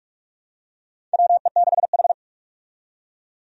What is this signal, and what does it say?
Morse code, heard as a keyed tone, about 47 wpm, 700 Hz. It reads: WE6H
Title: USA Call Signs - 4 Characters - 45wpm